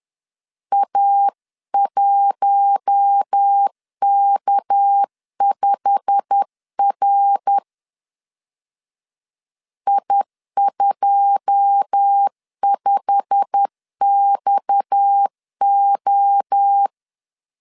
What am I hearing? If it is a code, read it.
Morse code, heard as a keyed tone, about 10 wpm, 800 Hz. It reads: A1K5R I25XO